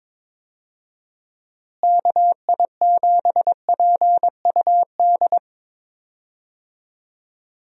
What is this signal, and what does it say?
Morse code, heard as a keyed tone, about 22 wpm, 700 Hz. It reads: KI7PUD
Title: USA Call Signs - 6 Characters - Repeated 2x - 22wpm